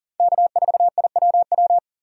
Morse code, heard as a keyed tone, about 40 wpm, 700 Hz. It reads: K4IWW
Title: USA Call Signs - 5 Characters - 40wpm